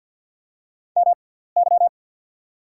Morse code, heard as a keyed tone, about 48 wpm, spaced as for 17 wpm, 700 Hz. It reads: MY
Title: 6x Character Spacing - Top 100 Words - 50wpm